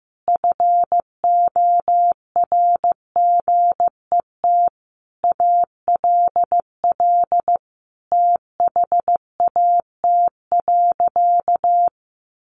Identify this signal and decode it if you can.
Morse code, heard as a keyed tone, about 15 wpm, 700 Hz. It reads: FORGET ALL THAT.
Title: Sentences from Top 1200 Words 15wpm